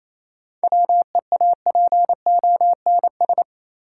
Morse code, heard as a keyed tone, about 28 wpm, 700 Hz. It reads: WEAPONS
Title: Sets of 5 Words Encore - New Word Drill - 28wpm